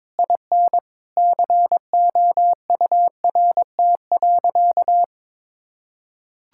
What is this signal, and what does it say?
Morse code, heard as a keyed tone, about 22 wpm, 700 Hz. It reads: IN COURT.
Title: Sentences from Top 1600 Words 22wpm